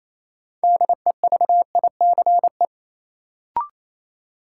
Morse code, heard as a keyed tone, about 28 wpm, 700 Hz. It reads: DEVICE
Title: Sentences from Top 1500 Words - New Word Drill - 28wpm